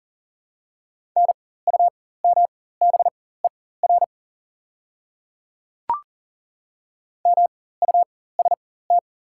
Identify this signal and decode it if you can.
Morse code, heard as a keyed tone, about 40 wpm, spaced as for 19 wpm, 700 Hz. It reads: NUMBER MUST
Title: 4x Character Spacing - Top 300 Words - 40wpm